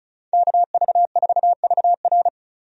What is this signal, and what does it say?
Morse code, heard as a keyed tone, about 36 wpm, 700 Hz. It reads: KV4VR
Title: USA Call Signs - 5 Characters - 35wpm